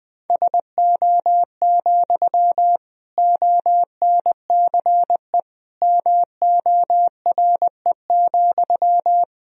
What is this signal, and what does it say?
Morse code, heard as a keyed tone, about 20 wpm, 700 Hz. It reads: SO, ONCE MORE,
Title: Sentences from Top 300 Words and Abbreviations 20wpm